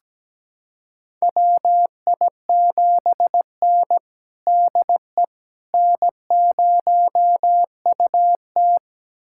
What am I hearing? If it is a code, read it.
Morse code, heard as a keyed tone, about 17 wpm, 700 Hz. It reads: WI7N DE N0UT
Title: USA Callsign DE Callsign - 4 Characters - 17wpm